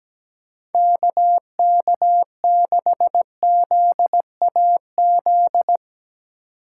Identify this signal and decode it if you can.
Morse code, heard as a keyed tone, about 17 wpm, 700 Hz. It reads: KK6ZAZ
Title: USA Call Signs - 3x Character Spacing - 17wpm